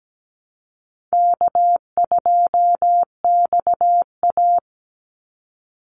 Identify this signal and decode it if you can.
Morse code, heard as a keyed tone, about 17 wpm, 700 Hz. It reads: K2XA
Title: Easy Call Signs Repeated 3x - 17wpm